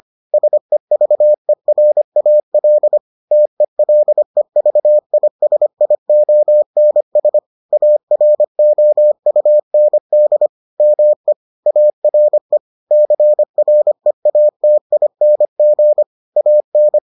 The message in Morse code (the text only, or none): SEVERAL TELEVISIONS AROUND ME ARE CREATING AN